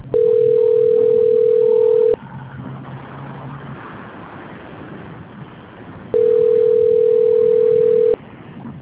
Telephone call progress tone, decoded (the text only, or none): ringback tone